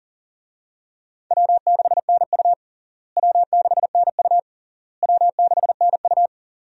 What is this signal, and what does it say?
Morse code, heard as a keyed tone, about 37 wpm, 700 Hz. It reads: W6NU W6NU W6NU
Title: USA Call Signs - 4 Characters - Repeated 3x - 40wpm